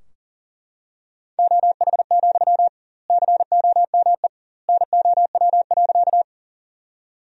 Morse code, heard as a keyed tone, about 40 wpm, 700 Hz. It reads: OH, COME NOW.
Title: 2x Word-Spacing - Sentences from Top 500 Words - 40wpm